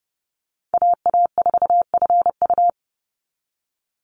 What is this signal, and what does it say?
Morse code, heard as a keyed tone, about 30 wpm, 700 Hz. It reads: AA4FU
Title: Call Signs Repeated 2x - 30wpm